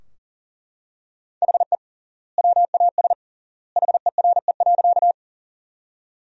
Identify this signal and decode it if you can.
Morse code, heard as a keyed tone, about 40 wpm, 700 Hz. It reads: HE WAS HERE.
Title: 3x Word-Spacing - Sentences from Top 200 Words - 40wpm